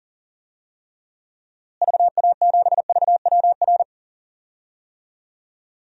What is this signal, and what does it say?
Morse code, heard as a keyed tone, about 40 wpm, 700 Hz. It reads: VA7VWR
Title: DXCC-100 Call Signs - 6 Characters - Repeated 2x - 40wpm